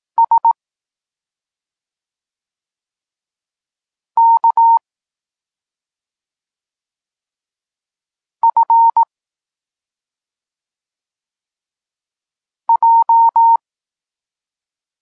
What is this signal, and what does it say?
Morse code, heard as a keyed tone, about 18 wpm, 920 Hz. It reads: S K F J
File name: Tekst-34 4